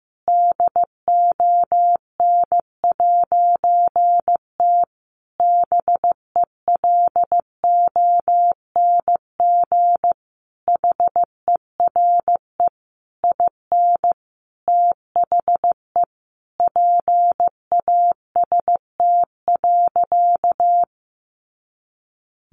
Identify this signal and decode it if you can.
Morse code, heard as a keyed tone, about 15 wpm, 700 Hz. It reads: DON'T BELONG HERE IN THE PAST.
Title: A Sound Of Thunder 15wpm